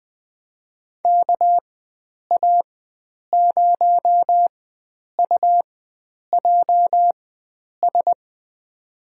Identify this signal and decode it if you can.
Morse code, heard as a keyed tone, about 20 wpm, spaced as for 9 wpm, 700 Hz. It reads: KA0UJS
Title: USA Call Signs - 6 Characters - 4x Character Spacing - 20wpm